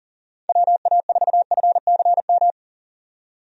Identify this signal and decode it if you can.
Morse code, heard as a keyed tone, about 40 wpm, 700 Hz. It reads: WA4FCM
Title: USA Call Signs - 6 Characters - 40wpm